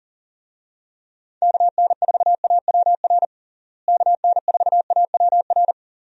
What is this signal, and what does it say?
Morse code, heard as a keyed tone, about 40 wpm, 700 Hz. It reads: KN4AWR KN4AWR
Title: USA Call Signs - 6 Characters - Repeated 2x - 40wpm